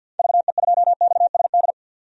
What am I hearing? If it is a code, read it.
Morse code, heard as a keyed tone, about 52 wpm, 700 Hz. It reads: VE2XSD
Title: DXCC-100 Call Signs - 6 Characters - 50wpm